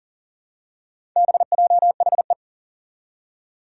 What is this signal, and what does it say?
Morse code, heard as a keyed tone, about 40 wpm, 700 Hz. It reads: BJHE
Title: Four Challenging Letter-Number Combinations 40wpm